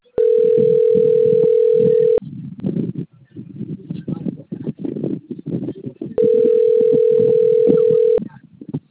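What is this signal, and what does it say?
Ringback tone (North America), 440 + 480 Hz.
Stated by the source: Outbound telephone call recorded on the Bebedouro 2 system.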